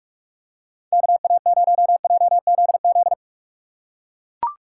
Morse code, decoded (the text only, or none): KA0JZZ